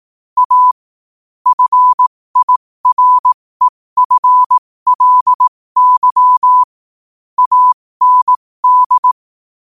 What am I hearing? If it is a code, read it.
Morse code, heard as a keyed tone, about 18 wpm, spaced as for 15 wpm, 1000 Hz. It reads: A FIREFLY AND